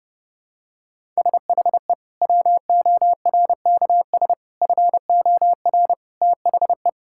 Morse code, heard as a keyed tone, about 32 wpm, 700 Hz. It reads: SHE WORKS FOR THE
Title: Sentences from Top 1000 Words - Review Part 5 - 30wpm